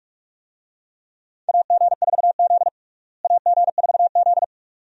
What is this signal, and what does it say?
Morse code, heard as a keyed tone, about 42 wpm, 700 Hz. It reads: AG4Z AG4Z
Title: USA Call Signs - 4 Characters - Repeated 2x - 45wpm